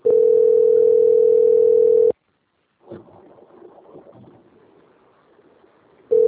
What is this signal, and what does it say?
Ringback tone (North America), 440 + 480 Hz.